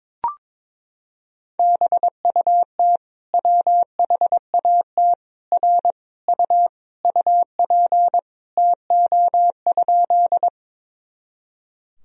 Morse code, heard as a keyed tone, about 22 wpm, 700 Hz. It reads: BUT WHAT R U UP TO?
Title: Sentences from Top 100 Words and Abbreviations 22wpm